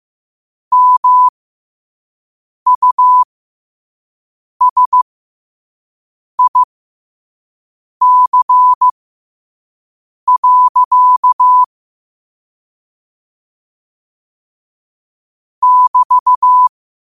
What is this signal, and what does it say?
Morse code, heard as a keyed tone, about 15 wpm, spaced as for 5 wpm, 1000 Hz. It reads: MUSIC. <BT>
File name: qotdfiles / 5wpm